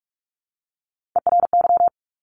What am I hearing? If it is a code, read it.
Morse code, heard as a keyed tone, about 46 wpm, 700 Hz. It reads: ERY
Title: Top Three Letter Combinations in Words 45wpm